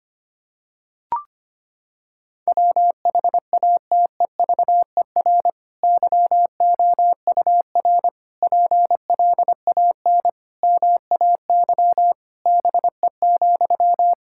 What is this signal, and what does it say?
Morse code, heard as a keyed tone, about 25 wpm, 700 Hz. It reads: WHATEVER YOUR PLAN MAY BE,
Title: Sentences from Top 1600 Words 25wpm